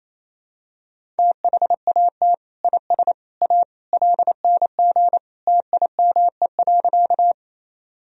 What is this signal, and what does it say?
Morse code, heard as a keyed tone, about 27 wpm, 700 Hz. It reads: THAT IS A LNG TIME.
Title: Sentences from Top 100 Words and Abbreviations 28wpm